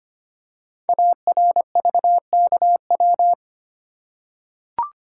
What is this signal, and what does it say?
Morse code, heard as a keyed tone, about 25 wpm, 700 Hz. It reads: ARVKW
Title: Code Groups - 5 Letters - 25wpm